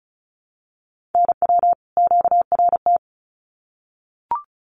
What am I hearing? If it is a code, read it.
Morse code, heard as a keyed tone, about 34 wpm, 700 Hz. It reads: NW QRT
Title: Common QSO Phrases 35wpm